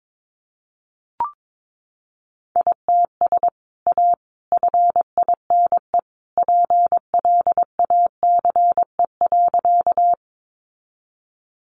Morse code, heard as a keyed tone, about 22 wpm, 700 Hz. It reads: ITS A FINE PLACE.
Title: Sentences from Top 600 Words 22wpm